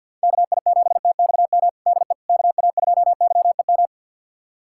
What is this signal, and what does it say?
Morse code, heard as a keyed tone, about 51 wpm, 700 Hz. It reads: KI7TXM DE KA2YEM
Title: USA Callsign DE Callsign - 6 Characters - 50wpm